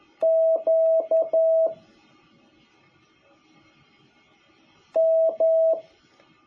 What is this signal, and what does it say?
Morse code, heard as a keyed tone, about 11 wpm, 650 Hz. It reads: Q M
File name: jh8cbh / 2021cw